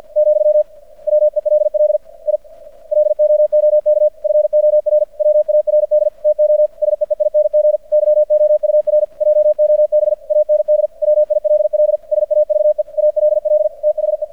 Morse code, heard as a keyed tone, about 50 wpm, 600 Hz. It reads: , WELL I LOOK FORWARD TO SEEING YOUR JOB AND WILL SAVE ALL THE
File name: archived / CRSnet